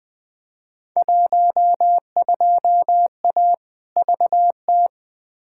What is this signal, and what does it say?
Morse code, heard as a keyed tone, about 20 wpm, 700 Hz. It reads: 12A VT
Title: Field Day Exchanges Encore - 20wpm